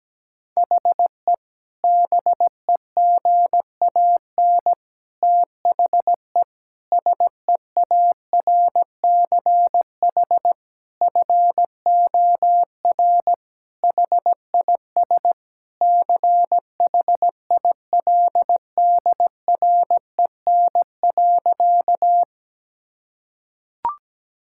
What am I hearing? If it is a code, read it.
Morse code, heard as a keyed tone, about 17 wpm, 700 Hz. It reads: HE BEGAN THE SEARCH FOR HIS CHILDREN.
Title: Sentences from Top 1000 Words - Review Part 5 - 17wpm